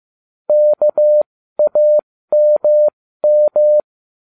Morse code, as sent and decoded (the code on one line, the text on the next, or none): -.- .- -- --
KAMM